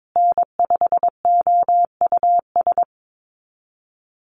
Morse code, heard as a keyed tone, about 22 wpm, 700 Hz. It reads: N5OUS
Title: USA Call Signs Repeated 2x 22wpm